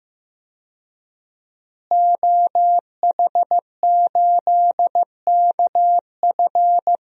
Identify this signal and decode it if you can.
Morse code, heard as a keyed tone, about 15 wpm, 700 Hz. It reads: OH8KF
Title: DXCC-100 Call Signs - 5 Characters - Repeated 3x - 15wpm